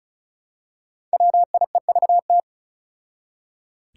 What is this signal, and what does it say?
Morse code, heard as a keyed tone, about 36 wpm, 700 Hz. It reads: WIEVT
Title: Five Challenging Letter-Number Combinations 35wpm